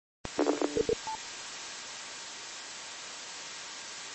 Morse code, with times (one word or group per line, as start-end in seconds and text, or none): none